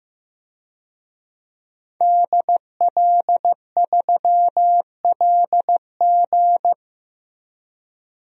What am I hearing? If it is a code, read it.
Morse code, heard as a keyed tone, about 15 wpm, 700 Hz. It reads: DL3LG
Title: DXCC-100 Call Signs - 5 Characters - 15wpm